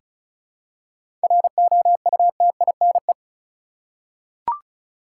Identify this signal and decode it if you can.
Morse code, heard as a keyed tone, about 36 wpm, 700 Hz. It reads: ROUTINE
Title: Seven Letter Words - 35wpm